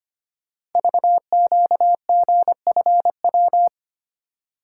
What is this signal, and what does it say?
Morse code, heard as a keyed tone, about 25 wpm, 700 Hz. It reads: VQGFW
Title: Code Groups - 5 Letters - 25wpm